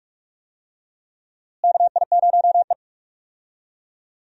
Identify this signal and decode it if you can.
Morse code, heard as a keyed tone, about 42 wpm, 700 Hz. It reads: KI0E
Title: USA Call Signs - 4 Characters - Repeated 3x - 45wpm